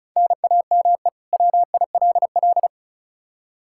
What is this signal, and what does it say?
Morse code, heard as a keyed tone, about 36 wpm, 700 Hz. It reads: NAME WILL
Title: K1USN SST QSO Practice 35wpm